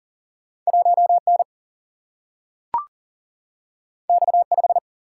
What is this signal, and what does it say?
Morse code, heard as a keyed tone, about 38 wpm, 700 Hz. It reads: 1 N X 5
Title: Single-Letter Single-Number 40wpm